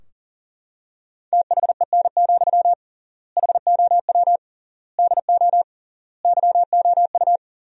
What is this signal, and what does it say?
Morse code, heard as a keyed tone, about 40 wpm, 700 Hz. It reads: THEN, HOW DO YOU
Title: 3x Word-Spacing - Sentences from Top 100 Words - 40wpm